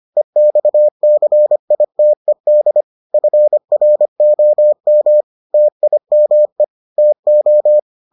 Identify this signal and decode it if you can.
Morse code, heard as a keyed tone, about 25 wpm, 600 Hz. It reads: EXCITED FROM TIME TO